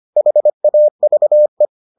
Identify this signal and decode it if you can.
Morse code, heard as a keyed tone, about 25 wpm, 600 Hz. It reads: HAVE